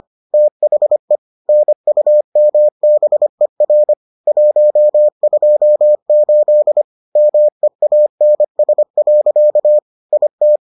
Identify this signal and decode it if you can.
Morse code, heard as a keyed tone, about 25 wpm, 600 Hz. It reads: THE NUMBER 128 MEANS. IT